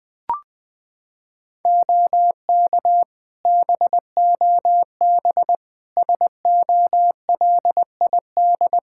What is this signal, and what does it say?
Morse code, heard as a keyed tone, about 20 wpm, 700 Hz. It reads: OK BOB SOLID
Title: Practice QSO - Exchange 3 - More Info - 20wpm